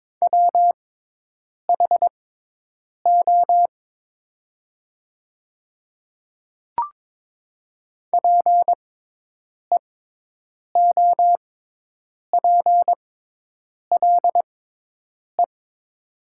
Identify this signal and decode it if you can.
Morse code, heard as a keyed tone, about 22 wpm, spaced as for 8 wpm, 700 Hz. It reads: WHO PEOPLE
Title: 6x Character Spacing - Top 100 Words - 22wpm